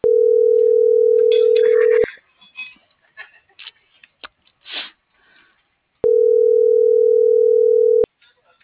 Ringback tone (North America), 440 + 480 Hz.